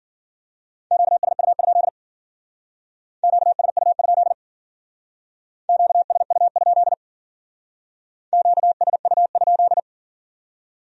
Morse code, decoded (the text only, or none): QSU? QSU? QSU? QSU?